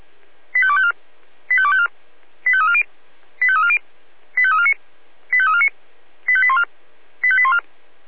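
Five-tone selective call (ZVEI), heard as quick runs of five tones, tones of about 70 ms. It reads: 85254 85254 85340 85340 85349 85349 85814 85814
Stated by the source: Five-tone ZVEI selective call received on an SDR.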